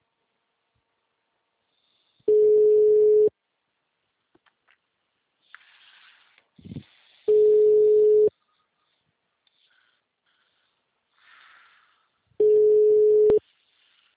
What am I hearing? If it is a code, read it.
Ringback tone (Europe, 425 Hz).